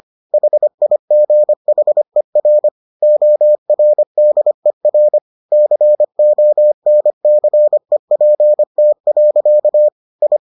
Morse code, heard as a keyed tone, about 25 wpm, 600 Hz. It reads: HIGHER ORDER CONCEPT. I